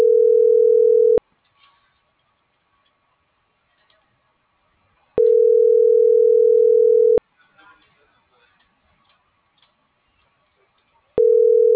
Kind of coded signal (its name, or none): telephone call progress tone